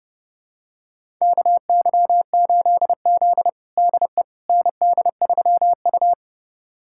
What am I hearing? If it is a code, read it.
Morse code, heard as a keyed tone, about 32 wpm, 700 Hz. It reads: KY8Z DE ND3U
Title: USA Callsign DE Callsign - 4 Characters - 30wpm